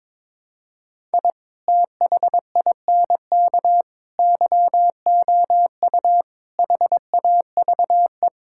Morse code, heard as a keyed tone, about 22 wpm, 700 Hz. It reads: I THINK YOU HAVE
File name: practice / 22wpm / Build A Phrase - 5-Word Phrases - 22wpm